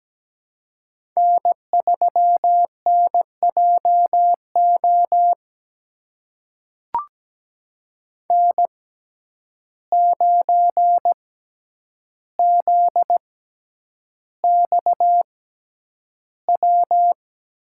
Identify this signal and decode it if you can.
Morse code, heard as a keyed tone, about 17 wpm, spaced as for 9 wpm, 700 Hz. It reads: N3NJO N9ZXW
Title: USA Call Signs - 5 Characters - 6x Character Spacing - 17wpm